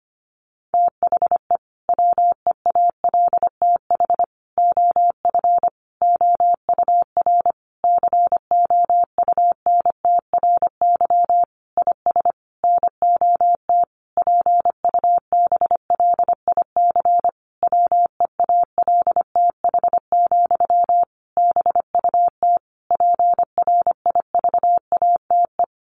Morse code, heard as a keyed tone, about 25 wpm, 700 Hz. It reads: THE WEALTH OF OUR COUNTRY IS NOT PUBLIC WEALTH, BUT PRIVATE
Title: Sentences from Top 1200 Words 25wpm